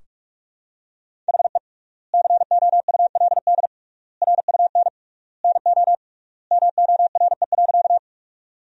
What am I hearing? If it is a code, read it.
Morse code, heard as a keyed tone, about 46 wpm, 700 Hz. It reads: HE COULD RUN NO MORE.
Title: 3x Word-Spacing - Sentences from Top 300 Words - 45wpm